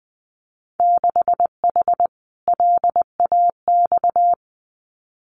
Morse code, heard as a keyed tone, about 20 wpm, 700 Hz. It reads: 6H LAX
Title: Winter Field Day Exchanges Encore 20wpm